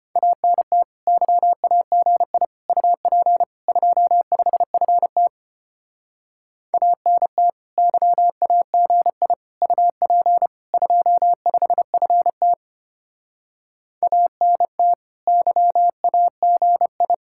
Morse code, heard as a keyed tone, about 32 wpm, 700 Hz. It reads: ANT YAGI UP 25FT ANT YAGI UP 25FT ANT YAGI